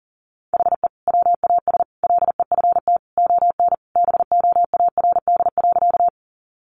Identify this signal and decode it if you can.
Morse code, heard as a keyed tone, about 37 wpm, 700 Hz. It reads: HE WAS LEFT ON BOARD.
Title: Sentences from Top 700 Words 40wpm